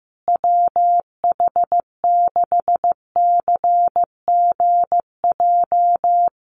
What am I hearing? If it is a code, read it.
Morse code, heard as a keyed tone, about 15 wpm, 700 Hz. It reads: WH6CGJ
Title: USA Call Signs Repeated 3x 15wpm